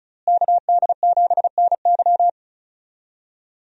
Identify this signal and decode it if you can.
Morse code, heard as a keyed tone, about 36 wpm, 700 Hz. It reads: KD7NY
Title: USA Call Signs - 5 Characters - Repeated 3x - 35wpm